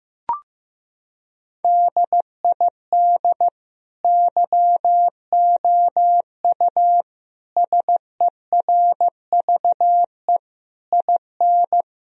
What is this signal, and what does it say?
Morse code, heard as a keyed tone, about 15 wpm, 700 Hz. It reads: DID YOU SERVE IN
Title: Sentences from Top 1000 Words - Review Part 4 - 15wpm